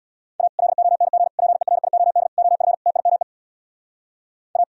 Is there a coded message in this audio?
Yes: Morse code from 0.4 to 4.6 s, heard as a keyed tone, about 64 wpm, 700 Hz. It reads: A LONG PERIOD OF TIME A